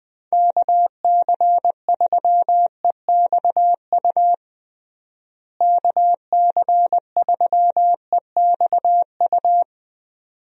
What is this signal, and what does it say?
Morse code, heard as a keyed tone, about 20 wpm, 700 Hz. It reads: KC3EXU KC3EXU
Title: USA Call Signs - 6 Characters - Repeated 2x - 20wpm